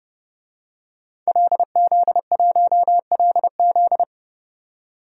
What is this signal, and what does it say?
Morse code, heard as a keyed tone, about 31 wpm, 700 Hz. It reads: LZ1LZ
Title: DXCC-100 Call Signs - 5 Characters - 30wpm